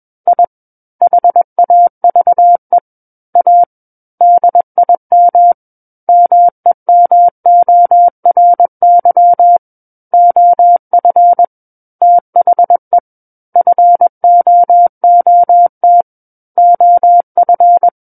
Morse code, heard as a keyed tone, about 21 wpm, 700 Hz. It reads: I HAVE A DIM MEMORY OF THE FOOT OF
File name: Book One / 12-Chapter 12 - 21 WPM